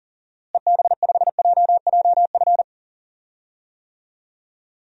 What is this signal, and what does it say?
Morse code, heard as a keyed tone, about 40 wpm, 700 Hz. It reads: EB5JJF
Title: DXCC-100 Call Signs - 6 Characters - Repeated 2x - 40wpm